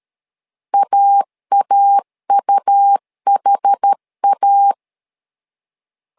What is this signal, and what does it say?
Morse code, heard as a keyed tone, about 13 wpm, 800 Hz. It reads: AAUHA
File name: LEZIONI / 2 UHDN / 2 UH 12wpm 1439